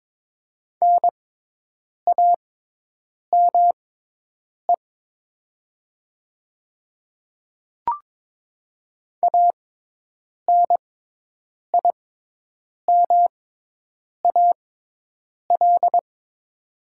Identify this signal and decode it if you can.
Morse code, heard as a keyed tone, about 22 wpm, spaced as for 8 wpm, 700 Hz. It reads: NAME ANIMAL
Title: 6x Character Spacing - Top 200 Words - 22wpm